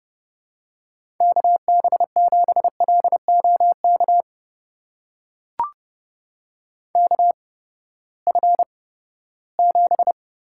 Morse code, heard as a keyed tone, about 32 wpm, spaced as for 17 wpm, 700 Hz. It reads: KB7LOK KF7
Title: USA Call Signs - 6 Characters - 8x Character Spacing - 30wpm